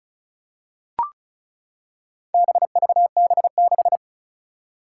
Morse code, heard as a keyed tone, about 36 wpm, 700 Hz. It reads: BVB6
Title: Four Challenging Letter-Number Combinations 35wpm